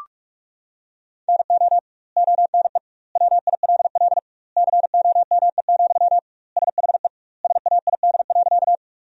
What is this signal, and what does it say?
Morse code, heard as a keyed tone, about 46 wpm, 700 Hz. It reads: NO ONE WILL COME, SHE SAID.
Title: 2x Word-Spacing - Sentences from Top 100 Words - 45wpm